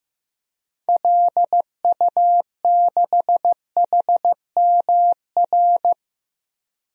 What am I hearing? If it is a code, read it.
Morse code, heard as a keyed tone, about 16 wpm, 700 Hz. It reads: LU6HMR
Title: DXCC-100 Call Signs - 6 Characters - Repeated 3x - 15wpm